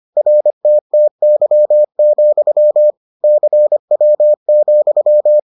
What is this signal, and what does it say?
Morse code, heard as a keyed tone, about 25 wpm, 600 Hz. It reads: RTTY, CW,